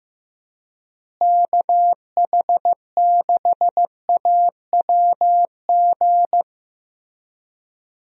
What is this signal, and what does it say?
Morse code, heard as a keyed tone, about 15 wpm, 700 Hz. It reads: KH6AWG
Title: DXCC-100 Call Signs - 6 Characters - Repeated 2x - 15wpm